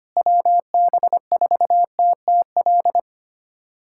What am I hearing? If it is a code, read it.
Morse code, heard as a keyed tone, about 25 wpm, 700 Hz. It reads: WB4TTL
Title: USA Callsign DE Callsign - 6 Characters - 25wpm